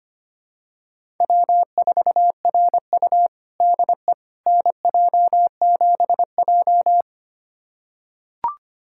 Morse code, heard as a keyed tone, about 25 wpm, 700 Hz. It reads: W4RU DE NJ7J
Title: USA Callsign DE Callsign - 4 Characters - 25wpm